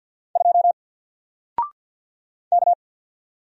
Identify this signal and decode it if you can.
Morse code, heard as a keyed tone, about 50 wpm, 700 Hz. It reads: 2 K
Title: Single Letter-Number 50wpm